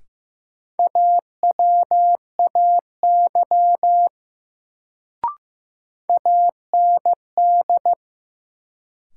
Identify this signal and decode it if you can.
Morse code, heard as a keyed tone, about 15 wpm, 700 Hz. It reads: AWAY AND
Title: Sets of 2 Words - New Word Drill - 15wpm